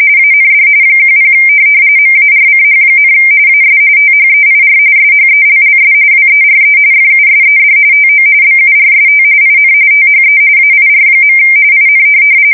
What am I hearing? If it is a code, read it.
RTTY, about 45.45 baud. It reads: AND 10 MARCH, AS THEIR SHELLS WERE FUSED TO BURST IN THE AIR. IN
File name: fsk170 / USS-Missouri